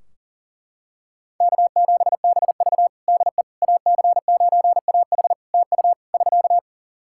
Morse code, heard as a keyed tone, about 40 wpm, 700 Hz. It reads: K7BV DE AC9AH TU <SK>
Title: Practice QSO - Exchange 4 - Ending - 40wpm